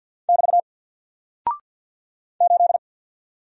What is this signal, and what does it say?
Morse code, heard as a keyed tone, about 48 wpm, 700 Hz. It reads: <BT> 8
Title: Single Character (Letter:Number:Punctuation:Prosign) 50wpm